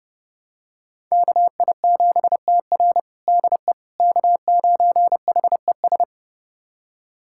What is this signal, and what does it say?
Morse code, heard as a keyed tone, about 30 wpm, 700 Hz. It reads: KI7TR DE K9HES
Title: USA Callsign DE Callsign - 5 Characters - 30wpm